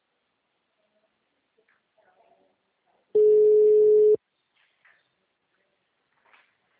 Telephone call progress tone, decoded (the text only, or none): ringback tone